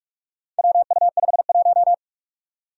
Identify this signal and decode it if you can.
Morse code, heard as a keyed tone, about 46 wpm, 700 Hz. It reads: WU51
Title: Four Challenging Letter-Number Combinations 45wpm